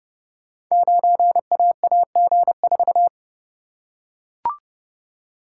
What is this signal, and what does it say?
Morse code, heard as a keyed tone, about 30 wpm, 700 Hz. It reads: 9AAG4
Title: Code Groups - 5 Letter:Numbers - 30wpm